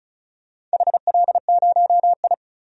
Morse code, heard as a keyed tone, about 34 wpm, 700 Hz. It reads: HL0I
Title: DXCC-100 Call Signs - 4 Characters - 35wpm